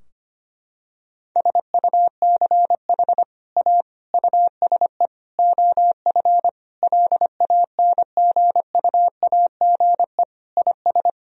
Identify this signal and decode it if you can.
Morse code, heard as a keyed tone, about 25 wpm, 700 Hz. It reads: SUCH A USE OF LANGUAGE IS
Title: Sentences from Top 1000 Words - Review Part 3 - 25wpm